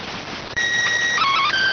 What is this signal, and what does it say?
SSTV (slow-scan television), PD 290.